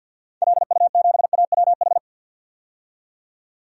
Morse code, heard as a keyed tone, about 52 wpm, 700 Hz. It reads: LU7AWH
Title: DXCC-100 Call Signs - 6 Characters - 50wpm